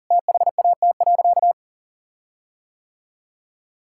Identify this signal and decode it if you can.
Morse code, heard as a keyed tone, about 37 wpm, 700 Hz. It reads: THAT.
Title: Sentences from Top 1600 Words 40wpm